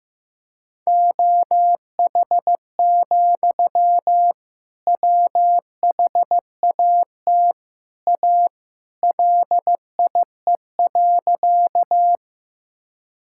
Morse code, heard as a keyed tone, about 15 wpm, 700 Hz. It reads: OH, WHAT A LIE.
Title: Sentences from Top 1600 Words 15wpm